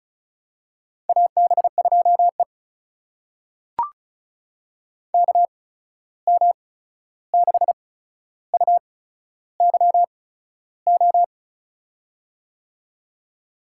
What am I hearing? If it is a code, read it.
Morse code, heard as a keyed tone, about 36 wpm, spaced as for 13 wpm, 700 Hz. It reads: AB2E KM6UYO
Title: USA Call Signs - 8x Character Spacing - 35wpm